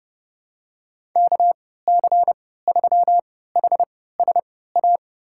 Morse code, heard as a keyed tone, about 30 wpm, spaced as for 17 wpm, 700 Hz. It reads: KC3HSA
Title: USA Call Signs - 6 Characters - 3x Character Spacing - 30wpm